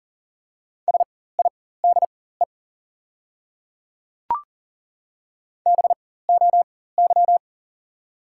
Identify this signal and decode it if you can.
Morse code, heard as a keyed tone, about 40 wpm, spaced as for 19 wpm, 700 Hz. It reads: SIDE BOY
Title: 4x Character Spacing - Top 200 Words - 40wpm